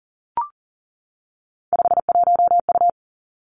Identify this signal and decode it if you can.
Morse code, heard as a keyed tone, about 41 wpm, 700 Hz. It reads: 51U
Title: Three Challenging Letter-Number Combinations 40wpm